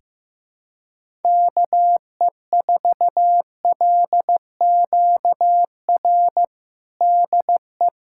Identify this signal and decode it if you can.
Morse code, heard as a keyed tone, about 15 wpm, 700 Hz. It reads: KE4LQR DE
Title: USA Callsign DE Callsign - 6 Characters - 15wpm